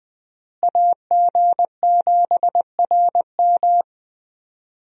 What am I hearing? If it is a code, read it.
Morse code, heard as a keyed tone, about 20 wpm, 700 Hz. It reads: AG7RM
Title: USA Call Signs - 5 Characters - Repeated 3x - 20wpm